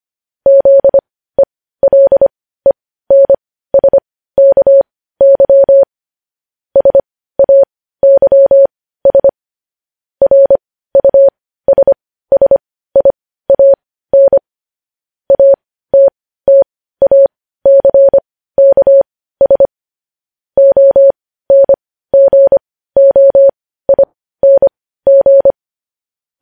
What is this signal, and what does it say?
Morse code, heard as a keyed tone, about 25 wpm, spaced as for 15 wpm, 550 Hz. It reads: ZELENSKY SAYS RUSSIAN ATTACKS ONGOING
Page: news headlines in morse code at 15 wpm